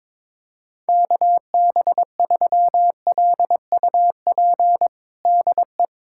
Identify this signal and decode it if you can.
Morse code, heard as a keyed tone, about 22 wpm, 700 Hz. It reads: KB3LUP DE
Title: USA Callsign DE Callsign 22wpm